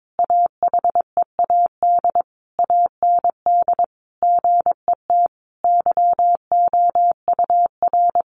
Morse code, heard as a keyed tone, about 22 wpm, 700 Hz. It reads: AHEAD AND GET YOUR
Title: Sentences from Top 1100 Words 22wpm